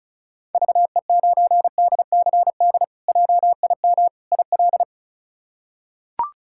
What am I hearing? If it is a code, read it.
Morse code, heard as a keyed tone, about 36 wpm, 700 Hz. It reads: VE9DCD JIM IL
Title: K1USN SST QSO Practice - 35wpm